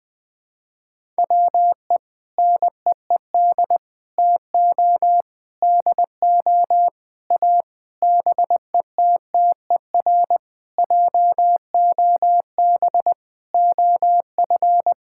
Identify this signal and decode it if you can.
Morse code, heard as a keyed tone, about 20 wpm, 700 Hz. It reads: WE NEED TO DO A BETTER JOB OF
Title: Sentences with 6-Word Phrases - 20wpm